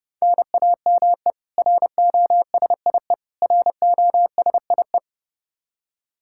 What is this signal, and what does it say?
Morse code, heard as a keyed tone, about 32 wpm, 700 Hz. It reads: NAME ROSIE ROSIE
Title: K1USN SST QSO Practice 30wpm